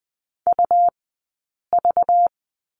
Morse code, heard as a keyed tone, about 20 wpm, 700 Hz. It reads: U V
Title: 190 Lesson 31d - Letter v - Two Characters 20wpm